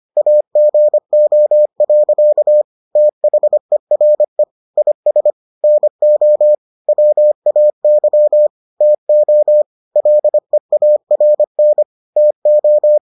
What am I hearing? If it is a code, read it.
Morse code, heard as a keyed tone, about 25 wpm, 600 Hz. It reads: AGO. THERE IS NO WAY TO LEARN TO